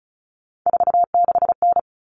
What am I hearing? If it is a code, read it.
Morse code, heard as a keyed tone, about 34 wpm, 700 Hz. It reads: 46N
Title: Common Cut Numbers - 3-Digit 35wpm